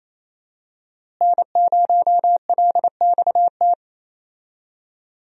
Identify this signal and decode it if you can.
Morse code, heard as a keyed tone, about 28 wpm, 700 Hz. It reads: N0LXT
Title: USA Call Signs - 5 Characters - Repeated 3x - 28wpm